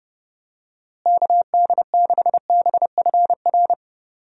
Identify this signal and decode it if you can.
Morse code, heard as a keyed tone, about 30 wpm, 700 Hz. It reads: KD6BFR
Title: USA Call Signs - 6 Characters - Repeated 2x - 30wpm